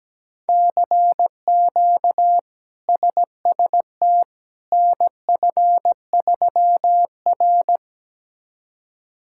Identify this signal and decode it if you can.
Morse code, heard as a keyed tone, about 17 wpm, 700 Hz. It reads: CQ SST NF3R
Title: K1USN SST QSO Practice - 17wpm